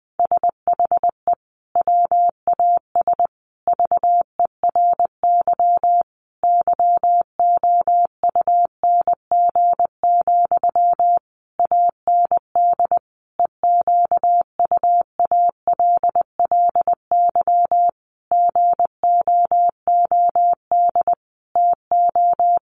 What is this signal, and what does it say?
Morse code, heard as a keyed tone, about 20 wpm, 700 Hz. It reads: SHE WAS VERY YOUNG, AND EQUALLY GOOD TO
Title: Sentences from Top 1400 Words 20wpm